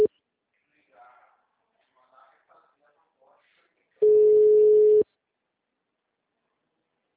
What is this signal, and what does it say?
Ringback tone (Europe, 425 Hz), with speech.